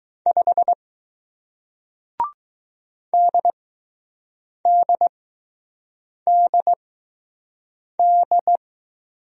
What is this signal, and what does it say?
Morse code, heard as a keyed tone, about 20 wpm, spaced as for 6 wpm, 700 Hz. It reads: 5 DDDD